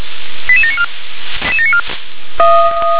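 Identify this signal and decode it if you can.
Five-tone selective call (ZVEI), heard as quick runs of five tones, tones of about 70 ms. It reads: 88794
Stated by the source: ZVEI five-tone call received by SDR.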